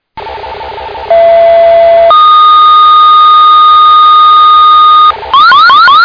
Two-tone page, heard as a one-second tone, then a three-second tone, about 670 then 1230 Hz. It is heard from 1.1 to 5.1 s.